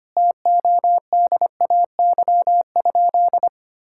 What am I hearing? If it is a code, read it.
Morse code, heard as a keyed tone, about 25 wpm, 700 Hz. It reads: TODAY?
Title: Sentences from Top 1600 Words 25wpm